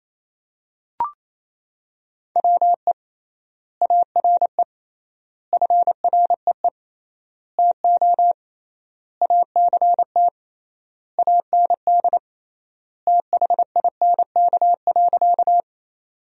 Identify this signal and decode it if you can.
Morse code, heard as a keyed tone, about 28 wpm, 700 Hz. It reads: WE ARE FREE TO ACT AND THINK.
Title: 3x Word-Spacing - Sentences from Top 500 Words - 28wpm